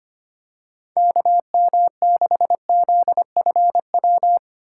Morse code, heard as a keyed tone, about 25 wpm, 700 Hz. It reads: KM6ZFW
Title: USA Call Signs - 6 Characters - 6x Character Spacing - 25wpm